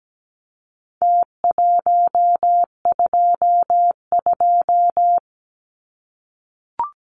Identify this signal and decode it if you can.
Morse code, heard as a keyed tone, about 17 wpm, 700 Hz. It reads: T122
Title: Numbers - Contest Style - 4-Digit 17wpm